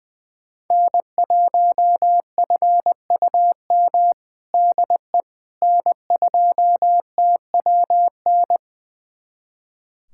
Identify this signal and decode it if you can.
Morse code, heard as a keyed tone, about 20 wpm, 700 Hz. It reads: N1FUM DE N2TWN
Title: USA Callsign DE Callsign - 5 Characters - 20wpm